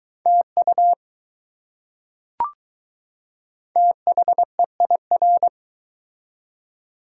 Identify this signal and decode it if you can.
Morse code, heard as a keyed tone, about 20 wpm, 700 Hz. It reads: TU THEIR